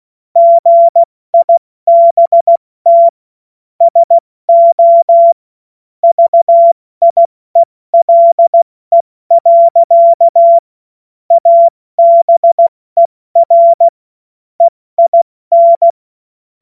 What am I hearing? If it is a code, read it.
Morse code, heard as a keyed tone, about 16 wpm, 680 Hz. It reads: GIBT SO VIELE. ABER EIN
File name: Morsekurs / Witze / Witz 08. November 2022 70 F80